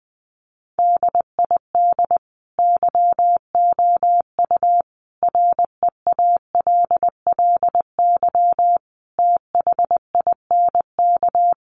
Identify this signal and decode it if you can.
Morse code, heard as a keyed tone, about 20 wpm, 700 Hz. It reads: DID YOU REALLY THINK
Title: Sentences from Top 1200 Words 20wpm